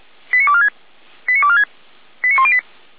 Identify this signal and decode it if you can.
Five-tone selective call (ZVEI), heard as quick runs of five tones, tones of about 80 ms. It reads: 89246 89246 89108